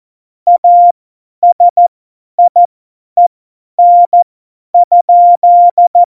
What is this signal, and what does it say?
Morse code, heard as a keyed tone, about 14 wpm, spaced as for 10 wpm, 700 Hz. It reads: ASIEN?
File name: Morsekurs / Witze / Witz 04. Juni 2019, 50 F70